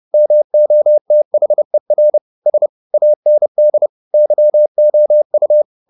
Morse code, heard as a keyed tone, about 30 wpm, 600 Hz. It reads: MOTHER S AND YOU
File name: the old man and the sea / the old man and the sea 1 30wpm